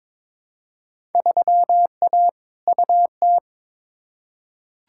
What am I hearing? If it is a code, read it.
Morse code, heard as a keyed tone, about 22 wpm, 700 Hz. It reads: 3A UT
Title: Field Day Exchanges - 22wpm